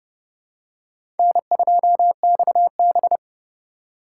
Morse code, heard as a keyed tone, about 30 wpm, 700 Hz. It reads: N2XB
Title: USA Call Signs - 4 Characters - 4x Character Spacing - 30wpm